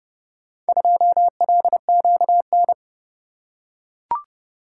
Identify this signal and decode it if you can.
Morse code, heard as a keyed tone, about 28 wpm, 700 Hz. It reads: 2LQN